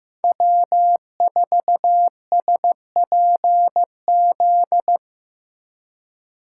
Morse code, heard as a keyed tone, about 15 wpm, 700 Hz. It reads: W4SPZ
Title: USA Call Signs - 3x Character Spacing - 15wpm